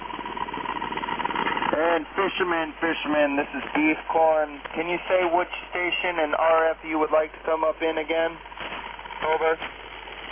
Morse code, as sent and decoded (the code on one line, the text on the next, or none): none